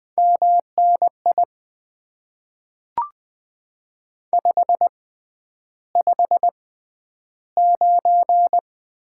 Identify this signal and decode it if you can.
Morse code, heard as a keyed tone, about 20 wpm, spaced as for 10 wpm, 700 Hz. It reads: MNI 559
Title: Top 90 QSO Elements - 6x Character Spacing - 20wpm